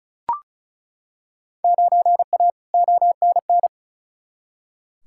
Morse code, heard as a keyed tone, about 36 wpm, 700 Hz. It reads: 9A ONN
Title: Field Day Exchanges Encore - 35wpm